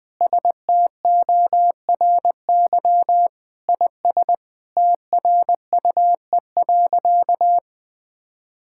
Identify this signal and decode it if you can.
Morse code, heard as a keyed tone, about 20 wpm, 700 Hz. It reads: STORY IS TRUE.
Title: Sentences with 7-Word Phrases - 20wpm